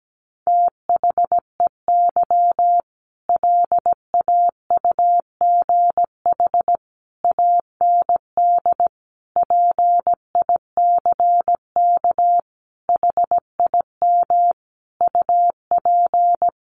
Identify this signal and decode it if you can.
Morse code, heard as a keyed tone, about 17 wpm, 700 Hz. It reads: THEY LAUGH AND PICK HIM UP
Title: Sentences from Top 1000 Words 17wpm